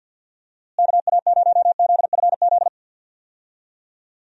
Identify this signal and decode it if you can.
Morse code, heard as a keyed tone, about 53 wpm, 700 Hz. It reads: KA0ZFZ
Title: USA Call Signs - 6 Characters - Repeated 2x - 50wpm